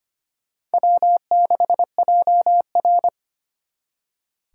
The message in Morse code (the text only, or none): W6JR